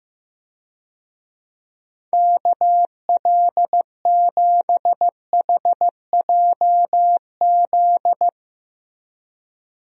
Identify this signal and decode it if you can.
Morse code, heard as a keyed tone, about 15 wpm, 700 Hz. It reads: KL7HJZ
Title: DXCC-100 Call Signs - 6 Characters - Repeated 2x - 15wpm